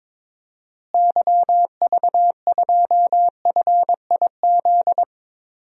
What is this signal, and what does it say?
Morse code, heard as a keyed tone, about 22 wpm, 700 Hz. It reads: YV2FIZ
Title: DXCC-100 Call Signs - 6 Characters - Repeated 2x - 22wpm